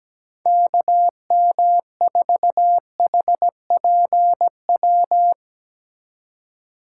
Morse code, heard as a keyed tone, about 17 wpm, 700 Hz. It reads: KM4HPW
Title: USA Call Signs - 6 Characters - 3x Character Spacing - 17wpm